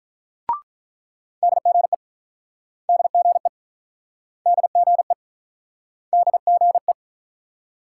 Morse code, heard as a keyed tone, about 46 wpm, 700 Hz. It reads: DGE DGE DGE DGE